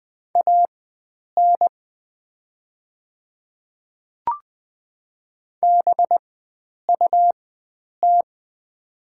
Morse code, heard as a keyed tone, about 20 wpm, spaced as for 9 wpm, 690 Hz. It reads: AN BUT
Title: 4x Character Spacing - Top 100 Words - 20wpm